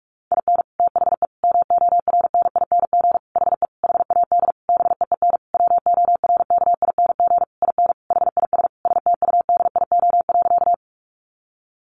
Morse code, heard as a keyed tone, about 44 wpm, 700 Hz. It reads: IN THE MORNING HE HAD BEEN WORKING IN HIS STUDIO.
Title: Sentences from Top 1200 Words 45wpm